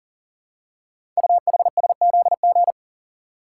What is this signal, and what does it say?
Morse code, heard as a keyed tone, about 39 wpm, 700 Hz. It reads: UHSZG
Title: Code Groups - 5 Letters - 40wpm